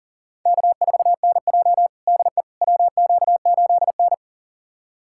Morse code, heard as a keyed tone, about 40 wpm, 700 Hz. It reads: K4NJ DE WQ8N
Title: USA Callsign DE Callsign - 4 Characters - 40wpm